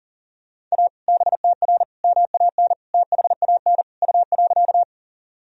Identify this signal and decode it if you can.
Morse code, heard as a keyed tone, about 40 wpm, 700 Hz. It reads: A BTR MAN THAN U.
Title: Sentences from Top 400 Words and Abbreviations 40wpm